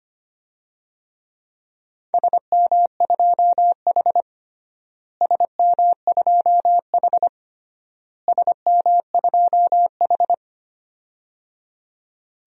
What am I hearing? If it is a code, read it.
Morse code, heard as a keyed tone, about 25 wpm, 700 Hz. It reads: SM2H SM2H SM2H
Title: DXCC-100 Call Signs - 4 Characters - Repeated 3x - 25wpm